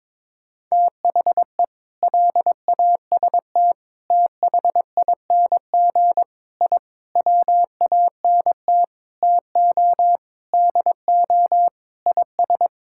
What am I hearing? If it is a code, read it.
Morse code, heard as a keyed tone, about 22 wpm, 700 Hz. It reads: THE LAST THING I WANT TO DO IS
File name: practice / 22wpm / Sentences with 7-Word Phrases - 22wpm